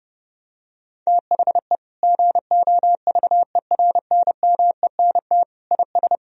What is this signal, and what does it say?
Morse code, heard as a keyed tone, about 32 wpm, 700 Hz. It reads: THE GOVERNMENT IS
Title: Sentences from Top 1000 Words - Review Part 4 - 30wpm